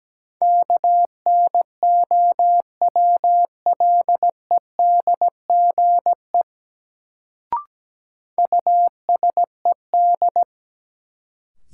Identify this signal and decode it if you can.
Morse code, heard as a keyed tone, about 17 wpm, 700 Hz. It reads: KNOWLEDGE USED
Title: Sentences from Top 600 Words - New Word Drill - 17wpm